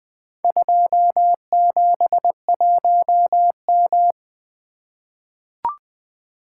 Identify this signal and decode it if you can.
Morse code, heard as a keyed tone, about 20 wpm, 700 Hz. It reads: 271M
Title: Four Letter-Number Combinations - 20wpm